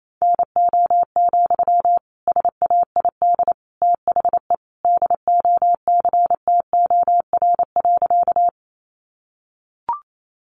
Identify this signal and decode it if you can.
Morse code, heard as a keyed tone, about 28 wpm, 700 Hz. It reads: NO, SAID THE DOCTOR.
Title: Sentences from Top 1500 Words 28wpm